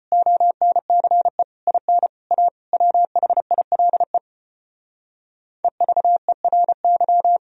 Morse code, heard as a keyed tone, about 33 wpm, 700 Hz. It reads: ONCE IN A WHILE EVERY